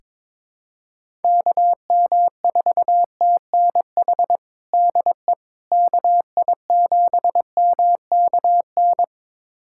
Morse code, heard as a keyed tone, about 22 wpm, 700 Hz. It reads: KM4TNH DE KI7MKN
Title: USA Callsign DE Callsign - 6 Characters - 22wpm